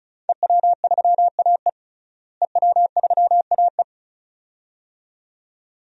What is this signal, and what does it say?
Morse code, heard as a keyed tone, about 36 wpm, 700 Hz. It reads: EW3AE EW3AE
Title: DXCC-100 Call Signs - 5 Characters - Repeated 2x - 35wpm